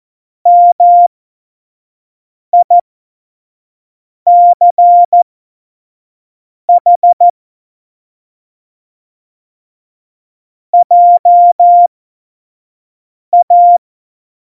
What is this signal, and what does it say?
Morse code, heard as a keyed tone, about 14 wpm, spaced as for 5 wpm, 700 Hz. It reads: MICH JA
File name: Morsekurs / Witze / Witz 25. Juni 2019, 25 F70